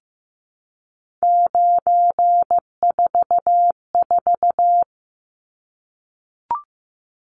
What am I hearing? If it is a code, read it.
Morse code, heard as a keyed tone, about 15 wpm, 700 Hz. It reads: 944
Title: Numbers - 3-Digit 15wpm